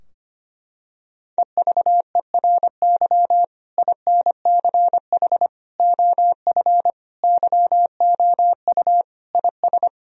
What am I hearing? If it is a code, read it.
Morse code, heard as a keyed tone, about 25 wpm, 700 Hz. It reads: EVERY INCH OF YOU IS
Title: Sentences from Top 1000 Words - Review Part 3 - 25wpm